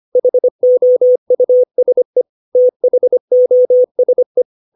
Morse code, heard as a keyed tone, about 25 wpm, 500 Hz. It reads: HOUSE THOSE
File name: 632 words / 632 Words 25WPM